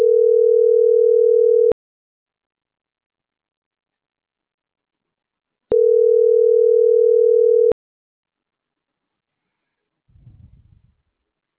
Ringback tone (North America), 440 + 480 Hz.
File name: CENTRAL MAERSK / out gabrielle.melendez - .50175